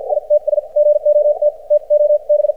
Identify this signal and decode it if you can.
Morse code, heard as a keyed tone, about 54 wpm, spaced as for 43 wpm, 590 Hz. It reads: ITS GOT TO B
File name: archived / CRSnet